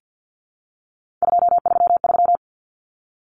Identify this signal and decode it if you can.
Morse code, heard as a keyed tone, about 48 wpm, 700 Hz. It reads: 233
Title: Common Cut Numbers - 3-Digit 50wpm